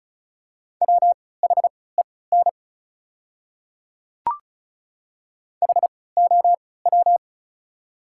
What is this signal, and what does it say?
Morse code, heard as a keyed tone, about 36 wpm, spaced as for 20 wpm, 700 Hz. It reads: WHEN HOW
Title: 3x Character Spacing - Top 100 Words - 35wpm